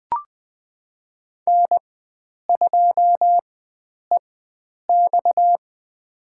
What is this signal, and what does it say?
Morse code, heard as a keyed tone, about 20 wpm, spaced as for 9 wpm, 700 Hz. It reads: N2EX
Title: USA Call Signs - 4 Characters - 4x Character Spacing - 20wpm